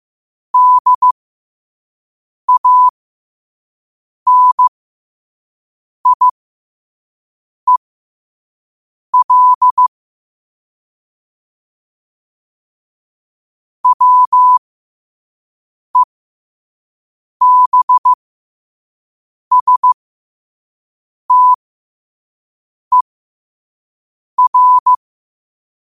Morse code, heard as a keyed tone, about 15 wpm, spaced as for 5 wpm, 1000 Hz. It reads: DANIEL WEBSTER
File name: qotdfiles / 5wpm